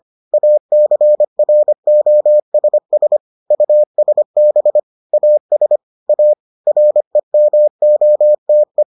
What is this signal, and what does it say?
Morse code, heard as a keyed tone, about 25 wpm, 600 Hz. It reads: ACROSS USB AS A REMOTE